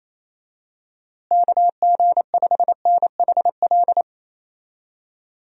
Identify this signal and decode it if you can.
Morse code, heard as a keyed tone, about 28 wpm, 700 Hz. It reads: KG5NHL
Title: USA Call Signs - 6 Characters - Repeated 2x - 28wpm